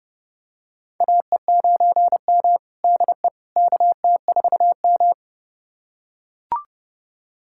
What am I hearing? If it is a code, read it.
Morse code, heard as a keyed tone, about 30 wpm, 700 Hz. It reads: AE9M DE KT4M
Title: USA Callsign DE Callsign - 4 Characters - 30wpm